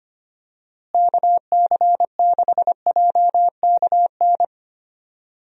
Morse code, heard as a keyed tone, about 25 wpm, 700 Hz. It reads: KC6JKN
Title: USA Call Signs - 6 Characters - Repeated 2x - 25wpm